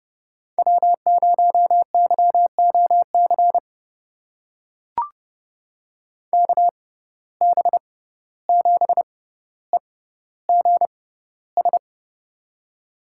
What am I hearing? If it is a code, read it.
Morse code, heard as a keyed tone, about 32 wpm, spaced as for 15 wpm, 700 Hz. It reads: W0YOC KB7EGS
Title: USA Call Signs - 6x Character Spacing - 30wpm